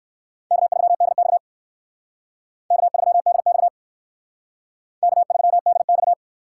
Morse code, heard as a keyed tone, about 64 wpm, 700 Hz. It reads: K3DX K3DX K3DX